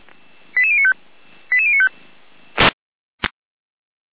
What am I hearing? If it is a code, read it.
Five-tone selective call (ZVEI), heard as quick runs of five tones, tones of about 70 ms. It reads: 88095 88095